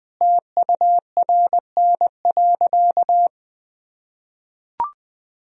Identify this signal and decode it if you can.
Morse code, heard as a keyed tone, about 20 wpm, 700 Hz. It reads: TURN.
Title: Sentences from Top 1000 Words - Review Part 1 - 20wpm